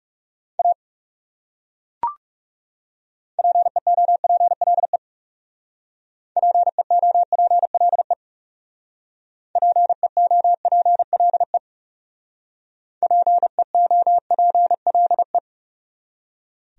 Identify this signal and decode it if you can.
Morse code, heard as a keyed tone, about 37 wpm, 700 Hz. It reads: A PEOPLE PEOPLE PEOPLE PEOPLE
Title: Speed-Racing Top 100 Words 30wpm